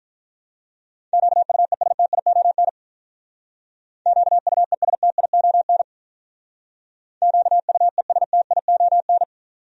Morse code, heard as a keyed tone, about 43 wpm, 700 Hz. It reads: QUESTION QUESTION QUESTION
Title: Speed-Racing Top 500 Words 35wpm